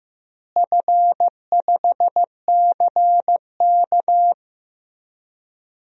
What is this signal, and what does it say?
Morse code, heard as a keyed tone, about 15 wpm, 700 Hz. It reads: F5CK
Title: DXCC-100 Call Signs - 4 Characters - Repeated 2x - 15wpm